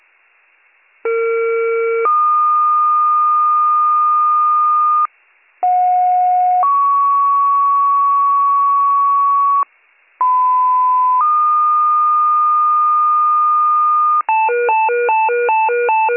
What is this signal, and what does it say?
Two-tone page, heard as a one-second tone, then a three-second tone, about 460 then 1220 Hz, about 730 then 1120 Hz, about 1010 then 1220 Hz. It is heard from 1.0 to 14.2 s.